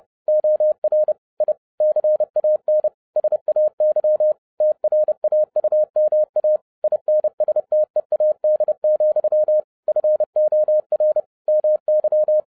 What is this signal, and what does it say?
Morse code, heard as a keyed tone, about 30 wpm, 610 Hz. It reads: OR I CAN SAY TRAUMA INSTEAD, FOR MY
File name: 2011-07-mp3 / n40 EXCITING QRP CW OPERATION 30wpm